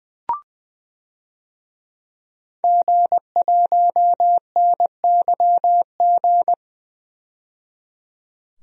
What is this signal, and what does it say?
Morse code, heard as a keyed tone, about 21 wpm, 700 Hz. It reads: G1NYG
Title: DXCC-100 Call Signs - 5 Characters - 20wpm